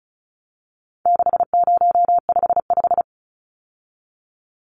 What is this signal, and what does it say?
Morse code, heard as a keyed tone, about 34 wpm, 700 Hz. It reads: 6055
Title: Numbers - 4-Digit 35wpm